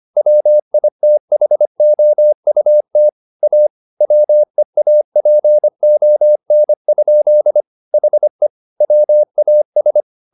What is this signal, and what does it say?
Morse code, heard as a keyed tone, about 25 wpm, 600 Hz. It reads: WITHOUT A WEAPON? HE WAS